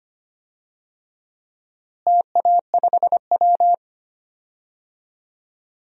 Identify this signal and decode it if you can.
Morse code, heard as a keyed tone, about 25 wpm, 700 Hz. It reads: TA5W
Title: DXCC-100 Call Signs - 4 Characters - Repeated 2x - 25wpm